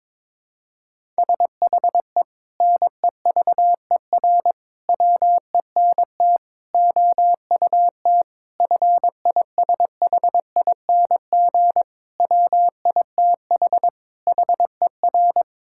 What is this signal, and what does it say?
Morse code, heard as a keyed tone, about 22 wpm, 700 Hz. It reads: SHE NEVER WENT OUT FISHING WITH HER
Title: Sentences from Top 1000 Words - Review Part 5 - 22wpm